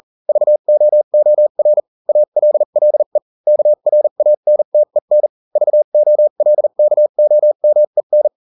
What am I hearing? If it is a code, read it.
Morse code, heard as a keyed tone, about 40 wpm, spaced as for 35 wpm, 600 Hz. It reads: VOOR ALLE KRANTEN VOLKOMEN